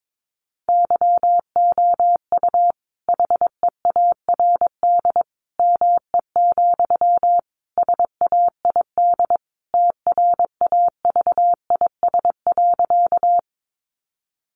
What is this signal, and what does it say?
Morse code, heard as a keyed tone, about 22 wpm, 700 Hz. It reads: YOU HEARD ME, SAID TRAVIS.
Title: A Sound Of Thunder 22wpm